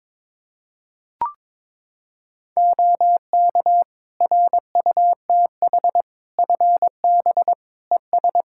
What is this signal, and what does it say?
Morse code, heard as a keyed tone, about 22 wpm, 700 Hz. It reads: OK RUTH FB ES
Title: Practice QSO - Exchange 2 - Info - 22wpm